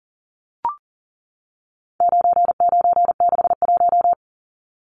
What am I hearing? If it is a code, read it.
Morse code, heard as a keyed tone, about 40 wpm, 700 Hz. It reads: 9961
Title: Numbers - 4-Digit 40wpm